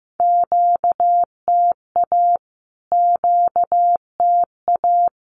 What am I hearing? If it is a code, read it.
Morse code, heard as a keyed tone, about 15 wpm, 700 Hz. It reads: QTA QTA
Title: Q-Codes Repeated 3x 15wpm